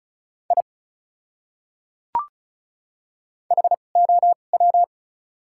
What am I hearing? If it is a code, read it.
Morse code, heard as a keyed tone, about 36 wpm, spaced as for 25 wpm, 700 Hz. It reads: I HOW
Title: 2x Character Spacing - Top 100 Words - 35wpm